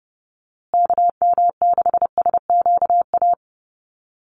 Morse code, heard as a keyed tone, about 30 wpm, 700 Hz. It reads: KM6SQA
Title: USA Call Signs Repeated 2x 30wpm